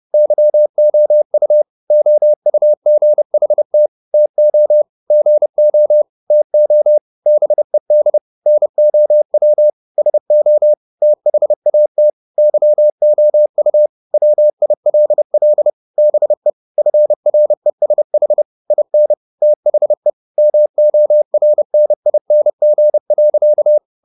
Morse code, heard as a keyed tone, about 30 wpm, 600 Hz. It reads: YOU OUGHT TO GO TO BED NOW SO THAT YOU WILL BE FRESH IN THE MORNING.
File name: the old man and the sea / the old man and the sea 1 30wpm